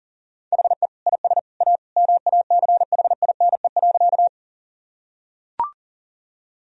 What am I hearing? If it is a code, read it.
Morse code, heard as a keyed tone, about 40 wpm, 700 Hz. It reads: HE IS A MACHINE.
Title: Sentences from Top 1000 Words - Review Part 2 - 40wpm